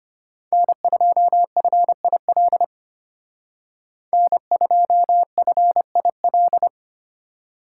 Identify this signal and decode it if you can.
Morse code, heard as a keyed tone, about 33 wpm, 700 Hz. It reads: N2FIL N2FIL